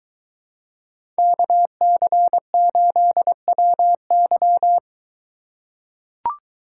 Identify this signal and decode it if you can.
Morse code, heard as a keyed tone, about 23 wpm, 700 Hz. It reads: KC8WY